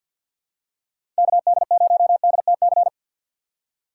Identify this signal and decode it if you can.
Morse code, heard as a keyed tone, about 53 wpm, 700 Hz. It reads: KD0DTC
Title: USA Call Signs - 6 Characters - 8x Character Spacing - 50wpm